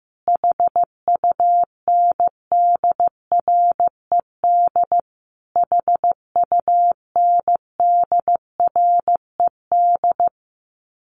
Morse code, heard as a keyed tone, about 15 wpm, 700 Hz. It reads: HUNDRED HUNDRED
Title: Top 500 Words Repeated 15wpm